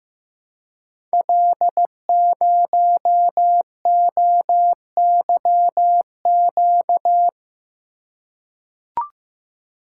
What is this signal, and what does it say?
Morse code, heard as a keyed tone, about 15 wpm, 700 Hz. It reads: L0OYQ
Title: Code Groups - 5 Letter:Numbers - 15wpm